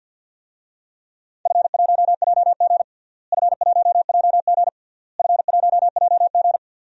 Morse code, heard as a keyed tone, about 50 wpm, 700 Hz. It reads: F1JG F1JG F1JG
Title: DXCC-100 Call Signs - 4 Characters - Repeated 3x - 50wpm